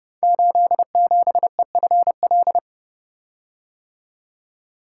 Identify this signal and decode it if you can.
Morse code, heard as a keyed tone, about 28 wpm, 700 Hz. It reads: 87EFL
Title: Code Groups - 5 Letter:Numbers - 30wpm